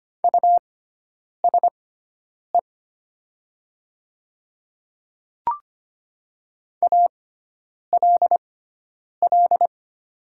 Morse code, heard as a keyed tone, about 25 wpm, spaced as for 9 wpm, 700 Hz. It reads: USE ALL
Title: 6x Character Spacing - Top 100 Words - 25wpm